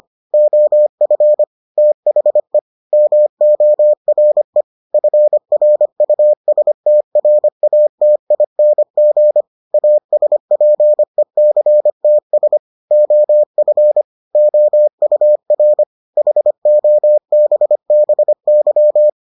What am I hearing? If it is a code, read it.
Morse code, heard as a keyed tone, about 25 wpm, 600 Hz. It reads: OF THE MORE FRUSTRATING ASPECTS OF OUR HOBBY